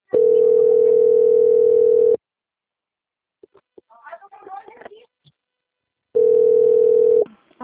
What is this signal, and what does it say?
Ringback tone (North America), 440 + 480 Hz.
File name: out- - - - - .4156